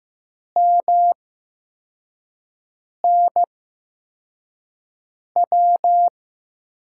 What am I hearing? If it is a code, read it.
Morse code, heard as a keyed tone, about 15 wpm, spaced as for 4 wpm, 700 Hz. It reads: MNW
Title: USA Call Signs - 6 Characters - 8x Character Spacing - 15wpm